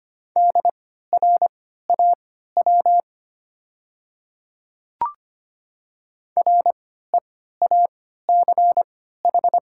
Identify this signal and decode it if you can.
Morse code, heard as a keyed tone, about 25 wpm, spaced as for 14 wpm, 690 Hz. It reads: DRAW REACH
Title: 3x Character Spacing - Top 500 Words - 25wpm